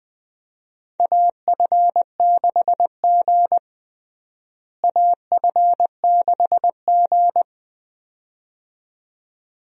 Morse code, heard as a keyed tone, about 20 wpm, 700 Hz. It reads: AF6G AF6G
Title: USA Call Signs - 4 Characters - Repeated 2x - 20wpm